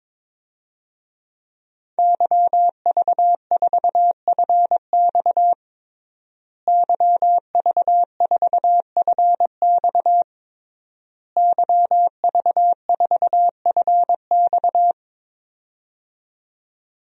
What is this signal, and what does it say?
Morse code, heard as a keyed tone, about 22 wpm, 700 Hz. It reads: YV4FX YV4FX YV4FX
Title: DXCC-100 Call Signs - 5 Characters - Repeated 3x - 22wpm